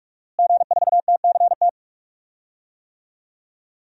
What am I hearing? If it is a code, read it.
Morse code, heard as a keyed tone, about 43 wpm, 700 Hz. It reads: G4TCT
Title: DXCC-100 Call Signs - 5 Characters - Repeated 2x - 45wpm